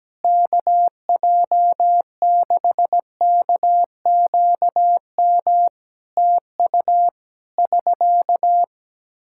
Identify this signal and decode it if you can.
Morse code, heard as a keyed tone, about 17 wpm, 700 Hz. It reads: KJ6KQM TU <SK>
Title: Practice QSO - Exchange 4 - Ending - 17wpm